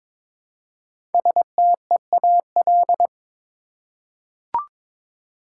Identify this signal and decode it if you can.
Morse code, heard as a keyed tone, about 22 wpm, 700 Hz. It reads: STEAL
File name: 5 Letter Words - First 3 Letters Given - 22wpm